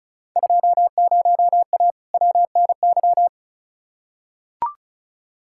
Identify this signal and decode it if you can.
Morse code, heard as a keyed tone, about 36 wpm, 690 Hz. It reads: 20A WNY
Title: Field Day Exchanges Encore - 35wpm